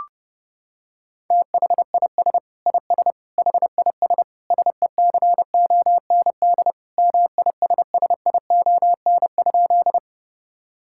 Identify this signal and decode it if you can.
Morse code, heard as a keyed tone, about 30 wpm, 700 Hz. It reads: THIS IS HIS SECOND MISSION?
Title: Sentences from Top 1600 Words 30wpm